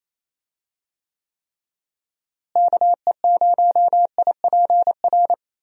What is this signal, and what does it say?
Morse code, heard as a keyed tone, about 27 wpm, 700 Hz. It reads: KE0IPR
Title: USA Call Signs - 6 Characters - Repeated 3x - 28wpm